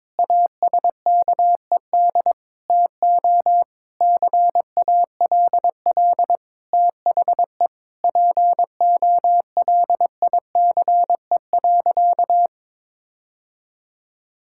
Morse code, heard as a keyed tone, about 22 wpm, 700 Hz. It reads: ASKED TO CALL THE POLICE.
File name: Sentences with 7-Word Phrases - 22wpm